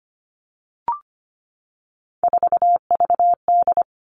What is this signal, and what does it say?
Morse code, heard as a keyed tone, about 25 wpm, 700 Hz. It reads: 4VD
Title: Three Challenging Letter-Number Combinations 25wpm